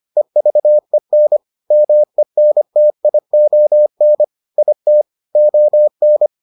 Morse code, heard as a keyed tone, about 25 wpm, 600 Hz. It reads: EVEN MENTION IT ON